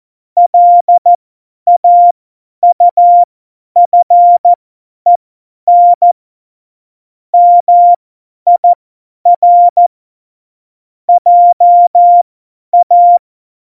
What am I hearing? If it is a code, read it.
Morse code, heard as a keyed tone, about 14 wpm, spaced as for 10 wpm, 700 Hz. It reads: LAUFEN MIR JA
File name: Morsekurs / Witze / Witz 08. Januar 2019, 50 F70